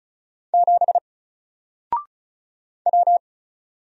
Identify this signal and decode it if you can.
Morse code, heard as a keyed tone, about 36 wpm, 700 Hz. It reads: 7 W
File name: Single-Letter-Number-35wpm